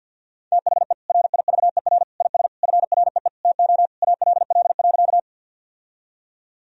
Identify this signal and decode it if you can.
Morse code, heard as a keyed tone, about 50 wpm, 700 Hz. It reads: THE RIVER IS FREE TO ALL.
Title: Sentences from Top 1000 Words - Review Part 1 - 50wpm